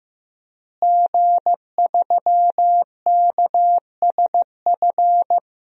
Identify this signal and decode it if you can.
Morse code, heard as a keyed tone, about 15 wpm, 700 Hz. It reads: G3KSF
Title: DXCC-100 Call Signs - 5 Characters - Repeated 3x - 15wpm